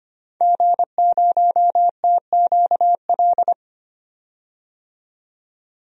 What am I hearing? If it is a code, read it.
Morse code, heard as a keyed tone, about 25 wpm, 700 Hz. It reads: G0TQL
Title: DXCC-100 Call Signs - 5 Characters - 25wpm